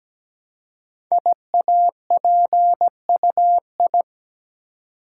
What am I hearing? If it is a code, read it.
Morse code, heard as a keyed tone, about 17 wpm, 700 Hz. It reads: IAPUI
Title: Code Groups - 5 Letters - 17wpm